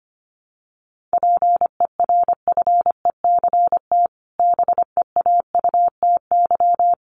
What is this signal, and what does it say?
Morse code, heard as a keyed tone, about 25 wpm, 700 Hz. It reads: PERFECT BEAUTY
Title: Sentences from Top 1200 Words 25wpm